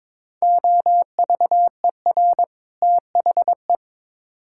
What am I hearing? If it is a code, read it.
Morse code, heard as a keyed tone, about 22 wpm, 700 Hz. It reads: OVER THE
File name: practice / 22wpm / Build A Phrase - 5-Word Phrases - 22wpm